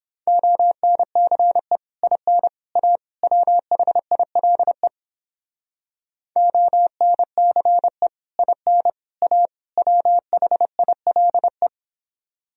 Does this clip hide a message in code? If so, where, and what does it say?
Yes: Morse code from 0.3 to 11.7 s, heard as a keyed tone, about 29 wpm, 700 Hz. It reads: ONCE IN A WHILE ONCE IN A WHILE